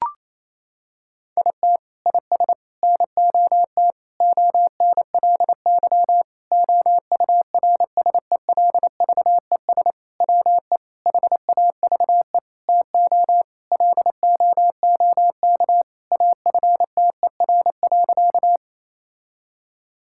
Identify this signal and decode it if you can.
Morse code, heard as a keyed tone, about 28 wpm, 700 Hz. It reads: IT IS NOT ONLY OURSELVES WE HAVE TO LOOK AFTER.
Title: Sentences from Top 1600 Words 28wpm